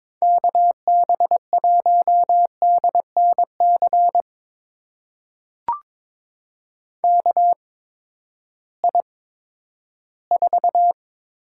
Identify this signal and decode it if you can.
Morse code, heard as a keyed tone, about 21 wpm, spaced as for 12 wpm, 700 Hz. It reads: KB1DNC KI4
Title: USA Call Signs - 6 Characters - 8x Character Spacing - 22wpm